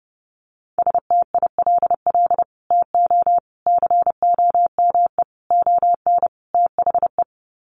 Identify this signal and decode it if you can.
Morse code, heard as a keyed tone, about 30 wpm, 700 Hz. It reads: STILL TO COME ON THE
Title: Sets of 5 Words Encore2 30wpm